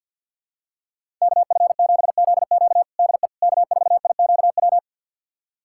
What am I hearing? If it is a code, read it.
Morse code, heard as a keyed tone, about 51 wpm, 700 Hz. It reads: KF7ZQ DE K4IQW
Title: USA Callsign DE Callsign - 5 Characters - 50wpm